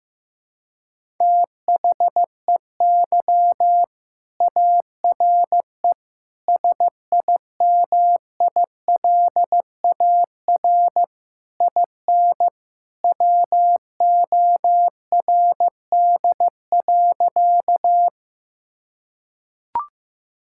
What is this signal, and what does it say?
Morse code, heard as a keyed tone, about 16 wpm, 700 Hz. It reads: THEY ARE SIMILAR IN WORD.
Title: Sentences from Top 1000 Words - Review Part 5 - 15wpm